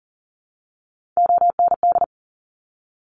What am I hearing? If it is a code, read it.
Morse code, heard as a keyed tone, about 40 wpm, 700 Hz. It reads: OND
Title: Top Three Letter Combinations in Words 40wpm